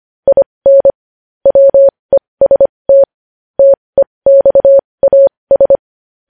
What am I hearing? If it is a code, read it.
Morse code, heard as a keyed tone, about 25 wpm, spaced as for 20 wpm, 550 Hz. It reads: IN WEST TEXAS
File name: podcast / News headlines-1x-25x20